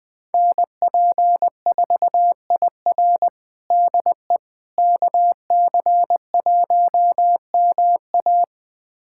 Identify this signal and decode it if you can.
Morse code, heard as a keyed tone, about 21 wpm, 700 Hz. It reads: NP4IR DE KC1MA
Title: USA Callsign DE Callsign - 5 Characters - 20wpm